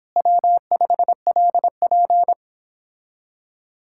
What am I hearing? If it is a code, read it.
Morse code, heard as a keyed tone, about 29 wpm, 700 Hz. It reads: W5LP